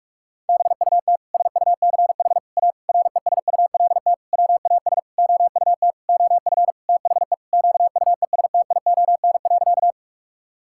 Morse code, heard as a keyed tone, about 46 wpm, 690 Hz. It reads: BUT SUCH A RESULT WAS OUT OF THE QUESTION.
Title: Sentences from Top 1000 Words - Review Part 5 - 45wpm